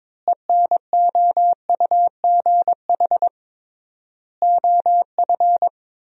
Morse code, heard as a keyed tone, about 22 wpm, 700 Hz. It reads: ENOUGH OF
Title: 3x Word-Spacing - Sentences from Top 400 Words - 22wpm